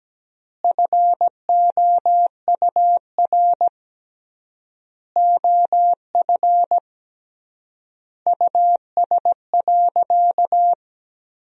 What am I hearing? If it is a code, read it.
Morse code, heard as a keyed tone, about 17 wpm, 700 Hz. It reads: FOUR OF US.
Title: 3x Word-Spacing - Sentences from Top 300 Words - 17wpm